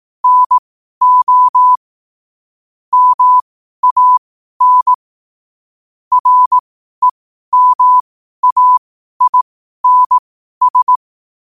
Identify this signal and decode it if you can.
Morse code, heard as a keyed tone, about 18 wpm, spaced as for 12 wpm, 1000 Hz. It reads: NO MAN REMAINS
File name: qotdfiles / 12wpm